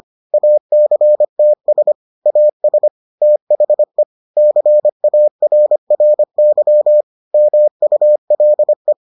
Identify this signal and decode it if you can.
Morse code, heard as a keyed tone, about 25 wpm, 600 Hz. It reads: ACTS AS THE CARRY MULE